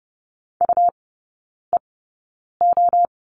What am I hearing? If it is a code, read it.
Morse code, heard as a keyed tone, about 30 wpm, spaced as for 9 wpm, 700 Hz. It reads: UEO
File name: Lesson 19e - Letter w - Three Characters 30wpm